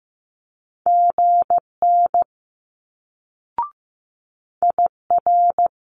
Morse code, heard as a keyed tone, about 15 wpm, 700 Hz. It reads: GN IR
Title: Multi-letter Phonemes Quick Fire 15wpm